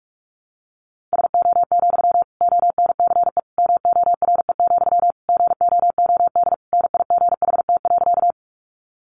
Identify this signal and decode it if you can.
Morse code, heard as a keyed tone, about 44 wpm, 700 Hz. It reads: SO, ONCE MORE, GOOD NIGHT.
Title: Sentences from Top 300 Words 45wpm